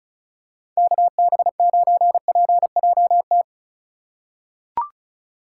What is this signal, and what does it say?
Morse code, heard as a keyed tone, about 36 wpm, 700 Hz. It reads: KB9PJT
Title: USA Call Signs - 2x Character Spacing - 35wpm